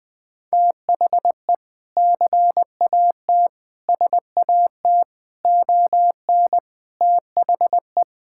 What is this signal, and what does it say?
Morse code, heard as a keyed tone, about 20 wpm, 700 Hz. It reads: THE CAT SAT ON THE
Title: Sentences with 2-Word Phrases - Spoken Beginning - 20wpm